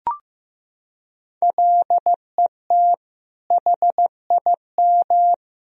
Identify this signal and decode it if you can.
Morse code, heard as a keyed tone, about 15 wpm, 700 Hz. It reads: LET HIM
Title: Sentences from Top 1000 Words - Review Part 2 - 15wpm